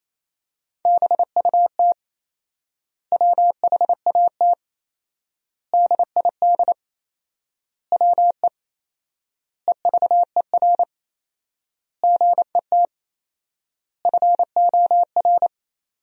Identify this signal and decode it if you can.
Morse code, heard as a keyed tone, about 28 wpm, 700 Hz. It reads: BUT WHAT DID WE EVER GET FOR
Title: 4x Word-Spacing - Sentences from Top 400 Words - 28wpm